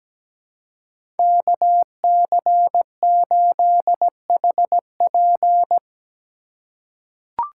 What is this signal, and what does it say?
Morse code, heard as a keyed tone, about 17 wpm, 700 Hz. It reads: KC8HP
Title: USA Call Signs - 5 Characters - 4x Character Spacing - 17wpm